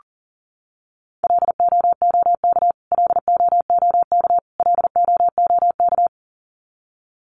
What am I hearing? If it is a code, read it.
Morse code, heard as a keyed tone, about 40 wpm, 700 Hz. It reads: LOOK LOOK LOOK
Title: Top 300 Words Repeated 40wpm